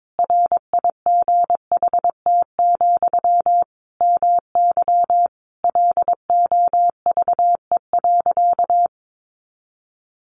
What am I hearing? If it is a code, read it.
Morse code, heard as a keyed tone, about 22 wpm, 700 Hz. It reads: RIGHT, MY LOVE.
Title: Sentences from Top 400 Words 22wpm